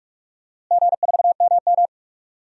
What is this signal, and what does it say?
Morse code, heard as a keyed tone, about 44 wpm, 700 Hz. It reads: G4MM
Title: DXCC-100 Call Signs - 4 Characters - Repeated 2x - 45wpm